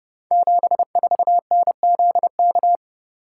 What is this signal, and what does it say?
Morse code, heard as a keyed tone, about 32 wpm, 700 Hz. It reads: 74NZK
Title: Code Groups - 5 Letter:Numbers - 30wpm